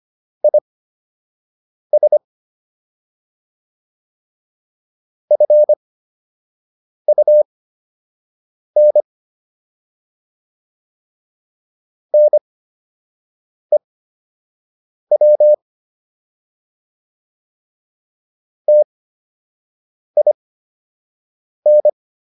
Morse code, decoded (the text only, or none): IS FUN NEW TIN